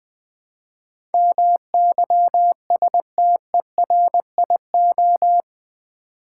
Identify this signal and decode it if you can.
Morse code, heard as a keyed tone, about 20 wpm, 700 Hz. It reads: MYSTERIO
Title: Build A Word - 4 Letters At A Time - 10-Letter Words - 20wpm